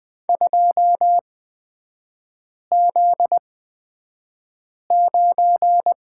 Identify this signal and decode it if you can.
Morse code, heard as a keyed tone, about 20 wpm, spaced as for 5 wpm, 700 Hz. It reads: 2Z9
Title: 243 Lesson 39e - Letter z - Three Characters 20wpm